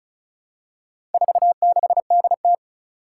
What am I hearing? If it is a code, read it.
Morse code, heard as a keyed tone, about 36 wpm, 700 Hz. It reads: 46DT
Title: Four Challenging Letter-Number Combinations 35wpm